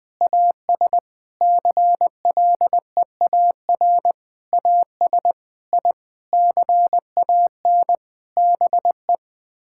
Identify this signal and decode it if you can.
Morse code, heard as a keyed tone, about 20 wpm, 700 Hz. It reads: AS CLEAR AS I CAN BE
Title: Sets of 9 Words - 20wpm